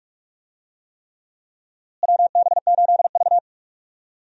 Morse code, heard as a keyed tone, about 46 wpm, 700 Hz. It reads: WB8V
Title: USA Call Signs - 4 Characters - Repeated 3x - 45wpm